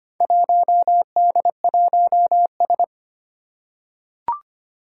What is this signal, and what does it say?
Morse code, heard as a keyed tone, about 25 wpm, 700 Hz. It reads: 1D1S
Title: Four Challenging Letter-Number Combinations 25wpm